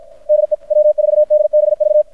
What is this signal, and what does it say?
Morse code, heard as a keyed tone, about 53 wpm, 600 Hz. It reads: DE W4DBV
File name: archived / CRSnet